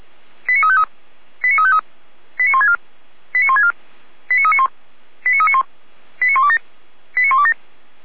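Five-tone selective call (ZVEI), heard as quick runs of five tones, tones of about 70 ms. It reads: 89353 89353 89164 89164 89391 89391 89127 89127
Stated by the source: ZVEI five-tone call 89117 received on SDR.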